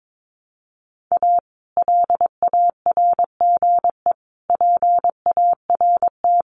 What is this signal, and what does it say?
Morse code, heard as a keyed tone, about 22 wpm, 700 Hz. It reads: A LARGE PART
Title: Sets of 5 Words Encore2 22wpm